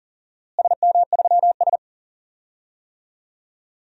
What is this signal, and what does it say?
Morse code, heard as a keyed tone, about 40 wpm, 700 Hz. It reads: SM3S
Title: DXCC-100 Call Signs - 4 Characters - Repeated 3x - 40wpm